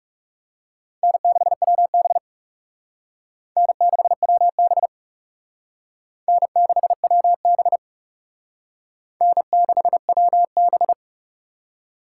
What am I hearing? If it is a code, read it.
Morse code, heard as a keyed tone, about 39 wpm, 700 Hz. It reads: N6WB N6WB N6WB N6WB